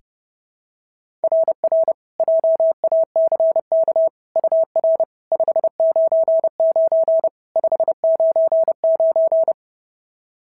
Morse code, heard as a keyed tone, about 32 wpm, 690 Hz. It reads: RR JACK UR 599 599
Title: SOTA QSO Practice 30wpm